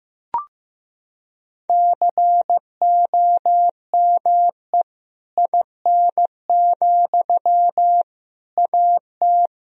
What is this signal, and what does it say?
Morse code, heard as a keyed tone, about 15 wpm, 700 Hz. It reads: COME IN, AT
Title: Sentences from Top 1000 Words - Review Part 4 - 15wpm